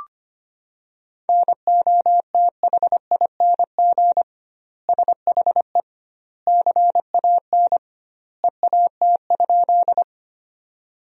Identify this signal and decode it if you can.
Morse code, heard as a keyed tone, about 25 wpm, 700 Hz. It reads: NOTHING SHE CAN EAT?
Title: 2x Word-Spacing - Sentences from Top 500 Words - 25wpm